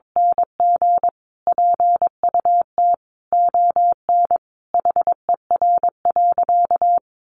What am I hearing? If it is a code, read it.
Morse code, heard as a keyed tone, about 22 wpm, 700 Hz. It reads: NG PUT ON HER.
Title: Sentences from Top 1300 Words 22wpm